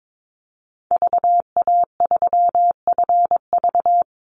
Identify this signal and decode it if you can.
Morse code, heard as a keyed tone, about 22 wpm, 700 Hz. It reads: VA3FV
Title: Call Signs Repeated 2x - 22wpm